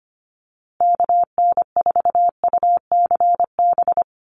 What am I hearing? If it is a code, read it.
Morse code, heard as a keyed tone, about 25 wpm, 700 Hz. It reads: KN4UCB
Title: USA Call Signs Repeated 2x 25wpm